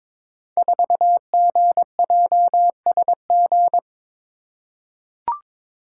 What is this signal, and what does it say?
Morse code, heard as a keyed tone, about 23 wpm, 700 Hz. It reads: 4GJSG
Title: Code Groups - 5 Letter:Numbers - 22wpm